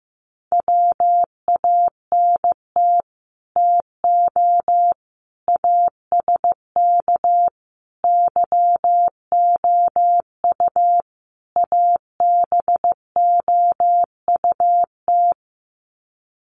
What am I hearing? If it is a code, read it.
Morse code, heard as a keyed tone, about 15 wpm, 700 Hz. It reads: WANT TO ASK YOU ABOUT
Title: Sets of 5 Words Encore2 15wpm